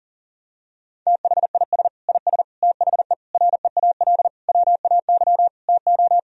Morse code, heard as a keyed tone, about 37 wpm, 700 Hz. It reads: THIS IS THE REAL WAY TO
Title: Sentences from Top 1000 Words - Review Part 2 - 40wpm